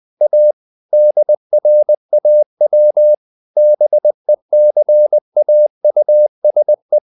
Morse code, heard as a keyed tone, about 20 wpm, 600 Hz. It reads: A DRAW BECAUSE
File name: the old man and the sea part 2 20wpm